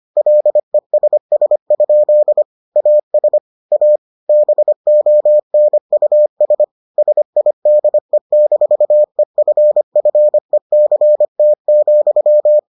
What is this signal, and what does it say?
Morse code, heard as a keyed tone, about 25 wpm, 600 Hz. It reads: LESS? AS A BONUS SIDE-EFFECT,